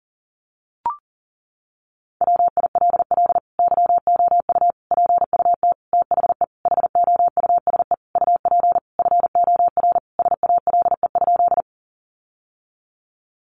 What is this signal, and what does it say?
Morse code, heard as a keyed tone, about 40 wpm, 700 Hz. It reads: WILL YOU PUT THE HOUSE UP FOR SALE?
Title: Sentences from Top 1400 Words 40wpm